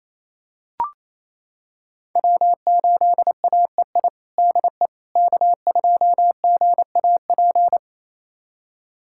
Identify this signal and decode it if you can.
Morse code, heard as a keyed tone, about 28 wpm, 700 Hz. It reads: W8AEI DE K2GAP
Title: USA Callsign DE Callsign - 5 Characters - 28wpm